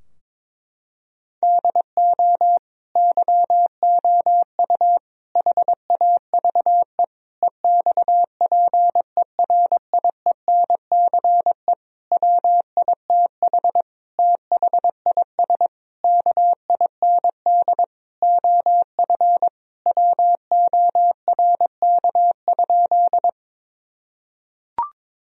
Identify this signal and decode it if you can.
Morse code, heard as a keyed tone, about 22 wpm, 700 Hz. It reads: DO YOU HAVE EXPERIENCE WITH THIS KIND OF WORK?
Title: Sentences from Top 1000 Words - Review Part 3 - 22wpm